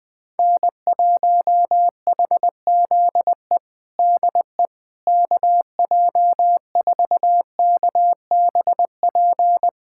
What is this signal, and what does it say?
Morse code, heard as a keyed tone, about 21 wpm, 700 Hz. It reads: N1HZE DE KJ4KBP
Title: USA Callsign DE Callsign 20wpm